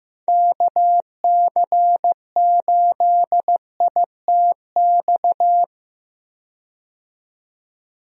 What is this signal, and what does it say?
Morse code, heard as a keyed tone, about 15 wpm, 700 Hz. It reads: KC8ITX
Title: USA Call Signs - 6 Characters - 15wpm